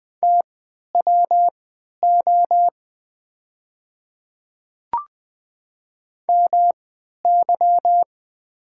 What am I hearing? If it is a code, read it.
Morse code, heard as a keyed tone, about 20 wpm, spaced as for 11 wpm, 700 Hz. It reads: TWO MY
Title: 3x Character Spacing - Top 100 Words - 20wpm